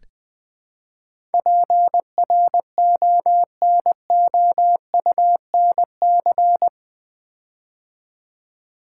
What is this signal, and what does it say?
Morse code, heard as a keyed tone, about 20 wpm, 700 Hz. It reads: PRONOUNC
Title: Build A Word - 4 Letters At A Time - 10-Letter Words - 20wpm